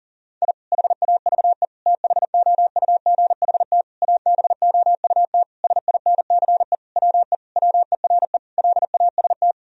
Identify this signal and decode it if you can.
Morse code, heard as a keyed tone, about 37 wpm, 700 Hz. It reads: I HAVE THOUGHT ABOUT SINCE WE WERE LAST
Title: Sentences from Top 1000 Words - Review Part 1 - 40wpm